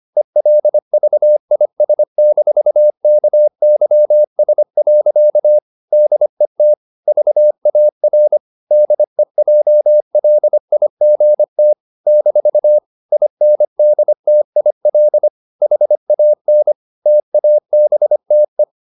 Morse code, heard as a keyed tone, about 25 wpm, 600 Hz. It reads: ELVIS-KYS. DET VAR DEJLIGT - INDTIL HAN TABTE
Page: Ferie 25 wpm